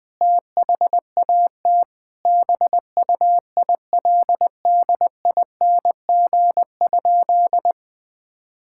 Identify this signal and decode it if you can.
Morse code, heard as a keyed tone, about 20 wpm, 700 Hz. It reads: THAT BUILDING?
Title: Sentences from Top 1000 Words - Review Part 5 - 20wpm